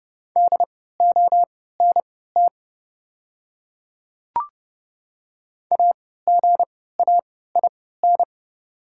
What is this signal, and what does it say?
Morse code, heard as a keyed tone, about 30 wpm, spaced as for 17 wpm, 690 Hz. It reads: DONT AGAIN
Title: 3x Character Spacing - Top 300 Words - 30wpm